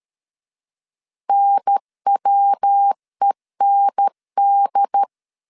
Morse code, heard as a keyed tone, about 13 wpm, 800 Hz. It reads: NWEND
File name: CW / DET / 1 3 1500C S B 12wpm 2828